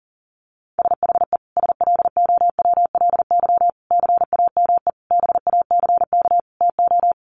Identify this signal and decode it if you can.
Morse code, heard as a keyed tone, about 37 wpm, 700 Hz. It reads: SHE SLOWLY CAME BACK TO
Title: Sentences from Top 1100 Words 40wpm